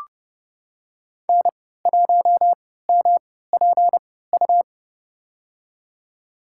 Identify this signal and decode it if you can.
Morse code, heard as a keyed tone, about 32 wpm, spaced as for 17 wpm, 700 Hz. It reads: N1MPU
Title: USA Call Signs - 5 Characters - 3x Character Spacing - 30wpm